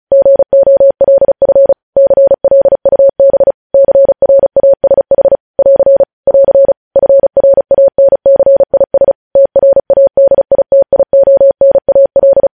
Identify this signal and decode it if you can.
Morse code, heard as a keyed tone, about 36 wpm, 550 Hz. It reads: GOLF CLUB CRASH <AR> <AR> FRANCIS TRADITIONAL
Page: News Headlines in Morse Code at 35 WPM